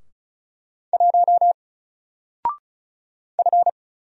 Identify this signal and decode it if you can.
Morse code, heard as a keyed tone, about 36 wpm, 700 Hz. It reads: 1 F
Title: Single Letter-Number Alt 35wpm